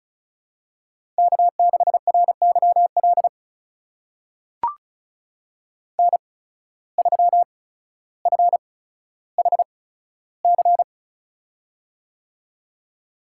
Morse code, heard as a keyed tone, about 36 wpm, spaced as for 15 wpm, 700 Hz. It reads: K6RYL N3FHC
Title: USA Call Signs - 5 Characters - 8x Character Spacing - 35wpm